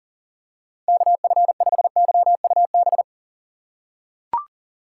Morse code, heard as a keyed tone, about 37 wpm, 700 Hz. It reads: KF5YUB
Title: K1USN SST QSO Practice - 40wpm